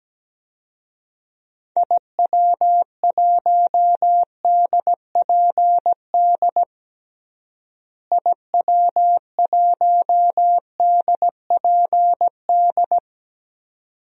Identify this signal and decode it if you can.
Morse code, heard as a keyed tone, about 17 wpm, 700 Hz. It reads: IW1DPD IW1DPD
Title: DXCC-100 Call Signs - 6 Characters - Repeated 2x - 17wpm